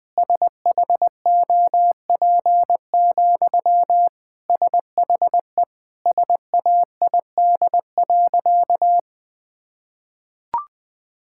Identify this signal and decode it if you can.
Morse code, heard as a keyed tone, about 20 wpm, 700 Hz. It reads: SHOP, SHE SAID.
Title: Sentences from Top 1600 Words 20wpm